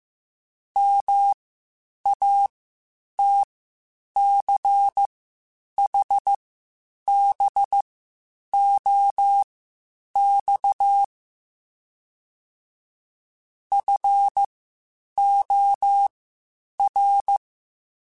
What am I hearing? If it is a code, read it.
Morse code, heard as a keyed tone, about 15 wpm, spaced as for 8 wpm, 800 Hz. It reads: MATCHBOX FOR
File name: QSOs / QSOs-mp3s / WD0FBT 1@13